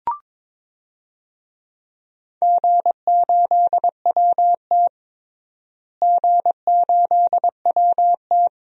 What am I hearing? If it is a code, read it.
Morse code, heard as a keyed tone, about 21 wpm, 700 Hz. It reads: G8WT G8WT
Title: DXCC-100 Call Signs - 4 Characters - Repeated 3x - 22wpm